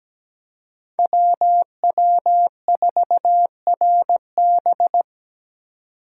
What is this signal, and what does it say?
Morse code, heard as a keyed tone, about 20 wpm, 700 Hz. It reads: WW4RB
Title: Speed Racing - USA 2x2 Call Signs 17wpm